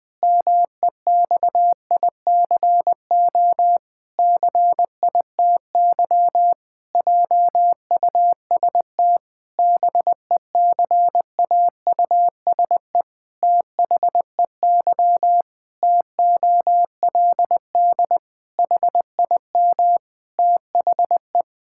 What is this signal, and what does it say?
Morse code, heard as a keyed tone, about 20 wpm, 700 Hz. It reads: MEXICO CITY JUST BECAUSE THEY TOLD HIM THE
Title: Pangram Challenge v2 - 20wpm